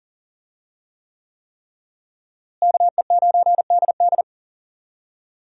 Morse code, heard as a keyed tone, about 40 wpm, 700 Hz. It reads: KE9DD
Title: USA Call Signs - 5 Characters - Repeated 3x - 40wpm